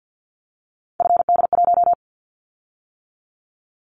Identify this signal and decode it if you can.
Morse code, heard as a keyed tone, about 50 wpm, 700 Hz. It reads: FD1
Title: Callsign Prefixes for Top 25 DXCC - 50wpm